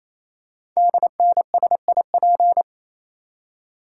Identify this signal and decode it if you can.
Morse code, heard as a keyed tone, about 29 wpm, 700 Hz. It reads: DNSIP
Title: Code Groups - 5 Letters - 28wpm